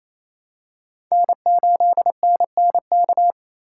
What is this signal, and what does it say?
Morse code, heard as a keyed tone, about 28 wpm, 700 Hz. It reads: N8NNK
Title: USA Call Signs - 5 Characters - 2x Character Spacing - 28wpm